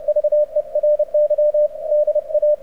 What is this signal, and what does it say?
Morse code, heard as a keyed tone, about 31 wpm, 590 Hz. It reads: VERY DA
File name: archived / CRSnet